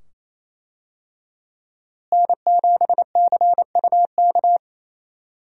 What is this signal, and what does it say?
Morse code, heard as a keyed tone, about 28 wpm, 700 Hz. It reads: N7CUK
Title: USA Call Signs - 5 Characters - Repeated 2x - 28wpm